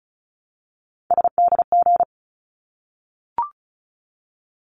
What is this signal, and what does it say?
Morse code, heard as a keyed tone, about 34 wpm, 700 Hz. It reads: SDG
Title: ARRL Sections 35wpm